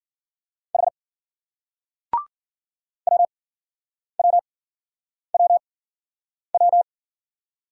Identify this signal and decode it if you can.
Morse code, heard as a keyed tone, about 56 wpm, spaced as for 9 wpm, 700 Hz. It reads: H WWWW